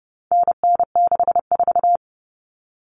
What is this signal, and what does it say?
Morse code, heard as a keyed tone, about 30 wpm, 700 Hz. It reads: NN64
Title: Common Cut Numbers - 4-Digit 30wpm